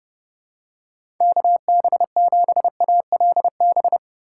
Morse code, heard as a keyed tone, about 30 wpm, 700 Hz. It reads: KB7ALB
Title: USA Call Signs - 3x Character Spacing - 30wpm